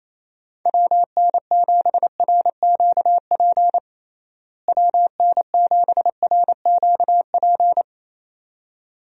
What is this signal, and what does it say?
Morse code, heard as a keyed tone, about 27 wpm, 700 Hz. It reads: WN7RQP WN7RQP
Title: USA Call Signs - 6 Characters - Repeated 2x - 28wpm